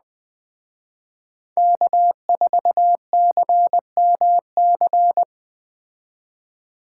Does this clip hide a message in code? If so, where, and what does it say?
Yes: Morse code from 1.6 to 5.2 s, heard as a keyed tone, about 20 wpm, 700 Hz. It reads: K4CMC